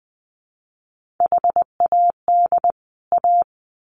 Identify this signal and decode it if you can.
Morse code, heard as a keyed tone, about 20 wpm, 700 Hz. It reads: HAD A
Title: Sets of 2 Words 20wpm